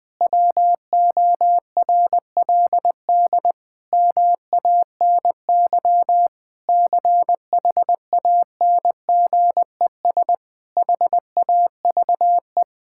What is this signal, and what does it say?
Morse code, heard as a keyed tone, about 20 wpm, 700 Hz. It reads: WORLD MANY CHANGES HAVE
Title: Sentences with 6-Word Phrases - Spoken Beginning - 20wpm